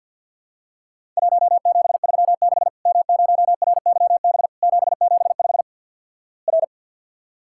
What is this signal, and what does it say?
Morse code, heard as a keyed tone, about 50 wpm, 700 Hz. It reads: 1736 M0RYB 775 R
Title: Worked All Europe DX Contest - QTC Exchange - 50wpm